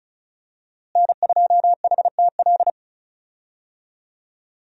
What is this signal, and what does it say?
Morse code, heard as a keyed tone, about 34 wpm, 700 Hz. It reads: N2HTL
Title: USA Call Signs - 5 Characters - Repeated 3x - 35wpm